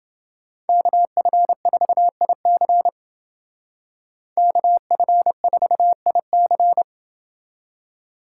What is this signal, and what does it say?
Morse code, heard as a keyed tone, about 28 wpm, 700 Hz. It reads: KF4IC KF4IC